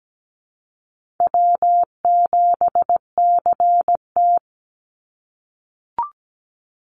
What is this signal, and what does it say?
Morse code, heard as a keyed tone, about 17 wpm, 700 Hz. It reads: W7CT
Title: Call Signs - 17wpm